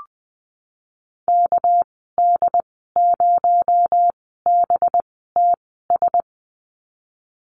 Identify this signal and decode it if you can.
Morse code, heard as a keyed tone, about 20 wpm, spaced as for 15 wpm, 700 Hz. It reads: KD0BTS
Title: 219 Lesson 35g - Number 0 - Call Signs 20wpm